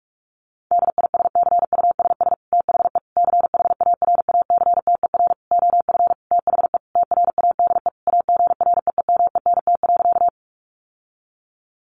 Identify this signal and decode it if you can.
Morse code, heard as a keyed tone, about 44 wpm, 700 Hz. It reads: DISCUSS THE CHARACTER OF THE TRADE AGREEMENT.
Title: Sentences from Top 1500 Words 45wpm